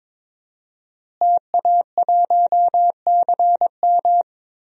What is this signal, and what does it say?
Morse code, heard as a keyed tone, about 22 wpm, 700 Hz. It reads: TA1CM
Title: DXCC-100 Call Signs - 5 Characters - Repeated 2x - 22wpm